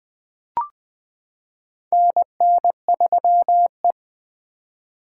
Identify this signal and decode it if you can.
Morse code, heard as a keyed tone, about 21 wpm, 700 Hz. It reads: NN3E
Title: 2023 POTA Activator Call Signs - 20wpm